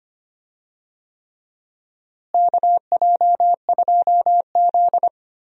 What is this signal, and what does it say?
Morse code, heard as a keyed tone, about 25 wpm, 700 Hz. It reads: KJ2Z
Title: USA Call Signs - 4 Characters - Repeated 3x - 25wpm